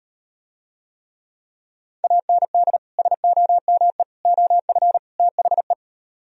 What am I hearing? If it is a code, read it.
Morse code, heard as a keyed tone, about 39 wpm, 690 Hz. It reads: AND SOME OF THE